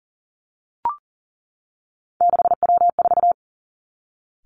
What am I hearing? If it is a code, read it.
Morse code, heard as a keyed tone, about 38 wpm, 700 Hz. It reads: 6W4
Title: Three Challenging Letter-Number Combinations 40wpm